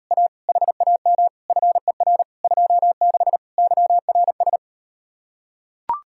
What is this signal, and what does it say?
Morse code, heard as a keyed tone, about 38 wpm, 700 Hz. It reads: A HAM FER 26 YRS